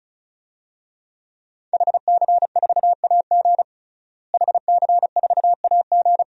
Morse code, heard as a keyed tone, about 34 wpm, 700 Hz. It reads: HC4AG HC4AG
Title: DXCC-100 Call Signs - 5 Characters - Repeated 2x - 35wpm